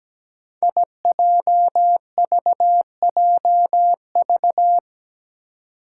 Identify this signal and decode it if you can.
Morse code, heard as a keyed tone, about 17 wpm, 700 Hz. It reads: IJVJV
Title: Five Challenging Letter-Number Combinations 17wpm